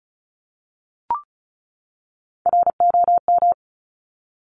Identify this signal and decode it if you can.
Morse code, heard as a keyed tone, about 36 wpm, 700 Hz. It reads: ROM
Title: Top Three Letter Combinations in Words 35wpm